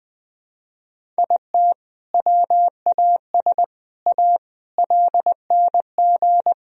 Morse code, heard as a keyed tone, about 20 wpm, 700 Hz. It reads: IT WAS A LNG
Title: Sentences from Top 500 Words and Abbreviations 20wpm